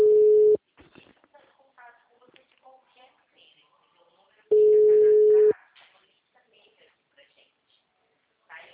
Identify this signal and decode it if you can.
Ringback tone (Europe, 425 Hz), with speech.
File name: out- -2011- - - .726